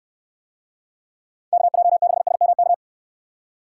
Morse code, heard as a keyed tone, about 69 wpm, 700 Hz. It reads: KJ6SMK